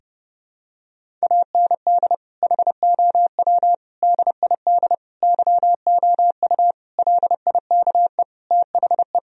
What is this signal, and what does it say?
Morse code, heard as a keyed tone, about 30 wpm, 700 Hz. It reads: AND HOW DID YOU LIKE THE
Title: Sentences from Top 1000 Words - Review Part 4 - 30wpm